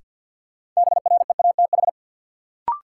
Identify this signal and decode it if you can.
Morse code, heard as a keyed tone, about 52 wpm, 700 Hz. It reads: BREATH
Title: Sets of 5 Words Encore - New Word Drill - 50wpm